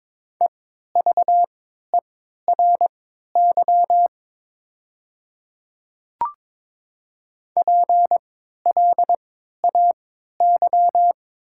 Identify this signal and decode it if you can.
Morse code, heard as a keyed tone, about 22 wpm, spaced as for 13 wpm, 700 Hz. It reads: EVERY PLAY
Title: 3x Character Spacing - Top 200 Words - 22wpm